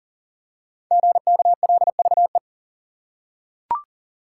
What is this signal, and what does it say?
Morse code, heard as a keyed tone, about 40 wpm, 690 Hz. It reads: GKLVE
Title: Code Groups - 5 Letter:Numbers - 40wpm